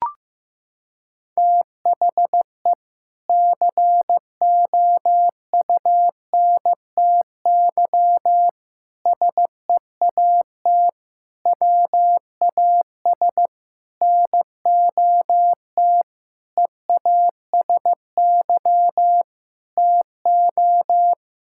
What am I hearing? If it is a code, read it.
Morse code, heard as a keyed tone, about 15 wpm, 700 Hz. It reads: THE COUNTY SEAT WAS NOT EASY TO
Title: Sentences from Top 1600 Words 15wpm